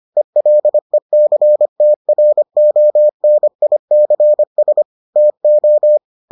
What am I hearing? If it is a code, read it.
Morse code, heard as a keyed tone, about 25 wpm, 600 Hz. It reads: ELECTRONICS TO